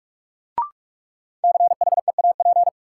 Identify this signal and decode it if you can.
Morse code, heard as a keyed tone, about 46 wpm, 700 Hz. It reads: CHEAP
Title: Sentences from Top 1200 Words - New Word Drill - 45wpm